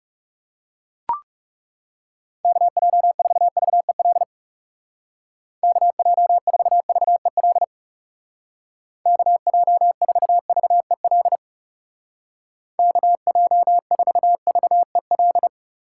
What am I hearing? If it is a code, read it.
Morse code, heard as a keyed tone, about 37 wpm, 700 Hz. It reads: KJ4VEL KJ4VEL KJ4VEL KJ4VEL
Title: Speed Racing - USA 2x3 Call Signs 30wpm